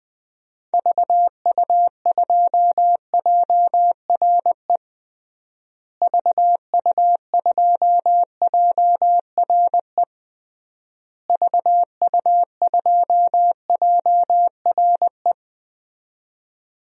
Morse code, heard as a keyed tone, about 20 wpm, 700 Hz. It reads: VU2JRE VU2JRE VU2JRE
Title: DXCC-100 Call Signs - 6 Characters - Repeated 3x - 20wpm